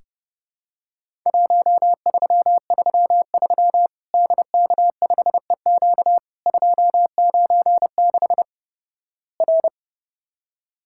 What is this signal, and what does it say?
Morse code, heard as a keyed tone, about 30 wpm, 700 Hz. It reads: 1333 DK5EQ 296 R
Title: Worked All Europe DX Contest - QTC Exchange - 30wpm